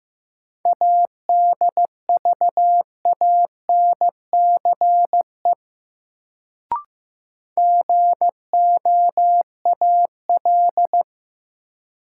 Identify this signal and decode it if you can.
Morse code, heard as a keyed tone, about 15 wpm, 700 Hz. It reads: ADVANCE GOAL
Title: Sentences from Top 1400 Words - New Word Drill Alt - 15wpm